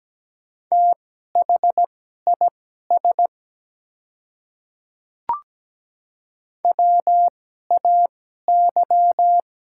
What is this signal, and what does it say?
Morse code, heard as a keyed tone, about 17 wpm, spaced as for 12 wpm, 700 Hz. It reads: THIS WAY
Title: 2x Character Spacing - Top 200 Words - 17wpm